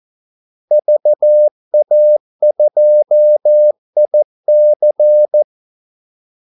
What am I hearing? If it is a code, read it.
Morse code, heard as a keyed tone, about 14 wpm, 600 Hz. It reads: VA2IC
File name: Callsigns-14wpm